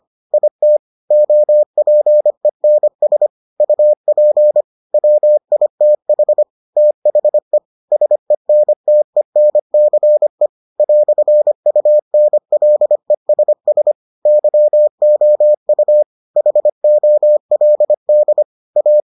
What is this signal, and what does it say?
Morse code, heard as a keyed tone, about 25 wpm, 600 Hz. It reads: IT OPENS UP WITH THE SENTENCE "UNLESS YOU HOLD A